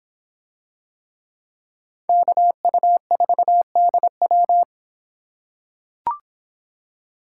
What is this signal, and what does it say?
Morse code, heard as a keyed tone, about 24 wpm, 700 Hz. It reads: KU4DW